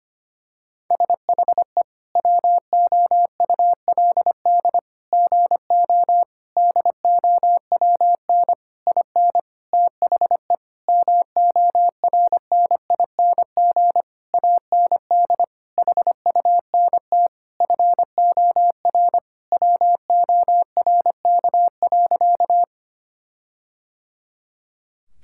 Morse code, heard as a keyed tone, about 25 wpm, 700 Hz. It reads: SHE WOULD GO DOWN IN THE MORNING AND HUNT FOR WORK.
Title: Sentences from Top 1600 Words 25wpm